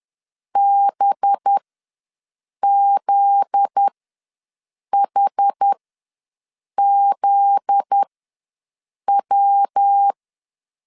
Morse code, heard as a keyed tone, about 10 wpm, spaced as for 6 wpm, 800 Hz. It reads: BZHZW
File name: LEZIONI / 7 GF9Z / 7 GF9Z ws 10wpm 3919